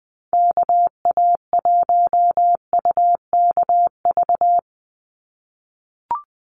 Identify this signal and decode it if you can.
Morse code, heard as a keyed tone, about 20 wpm, 700 Hz. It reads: KA1UKV
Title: USA Call Signs 20wpm